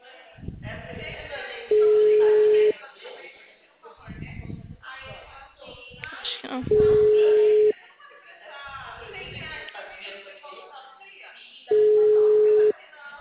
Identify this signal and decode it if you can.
Ringback tone (Europe, 425 Hz), with speech.
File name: out- -2016